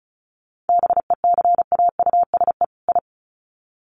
Morse code, heard as a keyed tone, about 34 wpm, 700 Hz. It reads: BECAUSE I
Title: Sets of 2 Words 35wpm